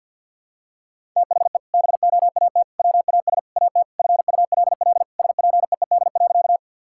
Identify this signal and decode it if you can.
Morse code, heard as a keyed tone, about 50 wpm, 700 Hz. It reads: THE BOAT WAS AT FULL SPEED.
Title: Sentences from Top 1000 Words - Review Part 5 - 50wpm